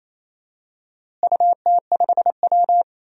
Morse code, heard as a keyed tone, about 27 wpm, 700 Hz. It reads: UT5W
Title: DXCC-100 Call Signs - 4 Characters - Repeated 2x - 28wpm